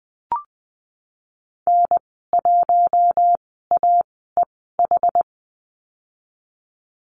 Morse code, heard as a keyed tone, about 20 wpm, spaced as for 15 wpm, 700 Hz. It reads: N1AEH
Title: 076 Lesson 13g - Letter L - Call Signs 20wpm